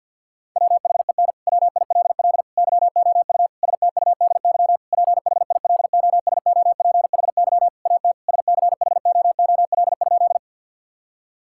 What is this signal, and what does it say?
Morse code, heard as a keyed tone, about 50 wpm, 700 Hz. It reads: WHEN WILL YOU STUDY PHILOSOPHY AT SCHOOL?
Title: Sentences from Top 1000 Words - Review Part 6 - 50wpm